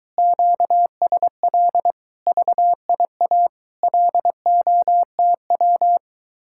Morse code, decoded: QSL VIA LOTW

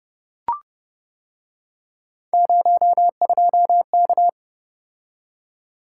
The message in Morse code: 02K

